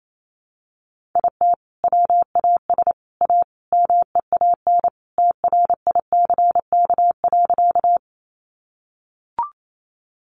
Morse code, heard as a keyed tone, about 28 wpm, 700 Hz. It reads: IT WAS A MEAN TRICK.
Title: Sentences from Top 1500 Words 28wpm